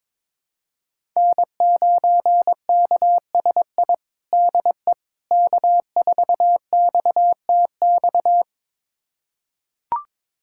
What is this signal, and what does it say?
Morse code, heard as a keyed tone, about 22 wpm, 700 Hz. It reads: N9KSI DE K4XTX
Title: USA Callsign DE Callsign - 5 Characters - 22wpm